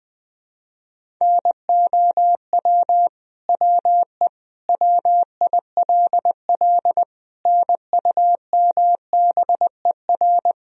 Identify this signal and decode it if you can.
Morse code, heard as a keyed tone, about 20 wpm, 700 Hz. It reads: NOW WE WILL NUMBER
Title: Sentences from Top 1000 Words - Review Part 1 - 20wpm